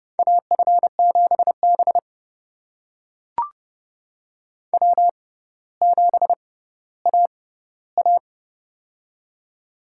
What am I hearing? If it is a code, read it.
Morse code, heard as a keyed tone, about 32 wpm, spaced as for 16 wpm, 700 Hz. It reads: AF7B W7AA
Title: USA Call Signs - 4 Characters - 6x Character Spacing - 30wpm